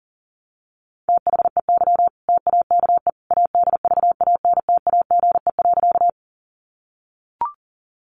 Morse code, heard as a keyed tone, about 40 wpm, 700 Hz. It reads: THEY TAKE ADVANTAGE.
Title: Sentences from Top 900 Words 40wpm